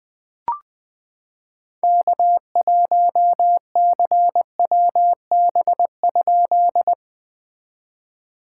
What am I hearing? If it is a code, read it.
Morse code, heard as a keyed tone, about 21 wpm, 700 Hz. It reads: K1CWB?
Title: POTA QSO Practice - 20wpm